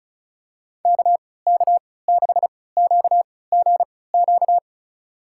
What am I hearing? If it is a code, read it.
Morse code, heard as a keyed tone, about 36 wpm, spaced as for 20 wpm, 700 Hz. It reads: KK6QGQ
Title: USA Call Signs - 6 Characters - 3x Character Spacing - 35wpm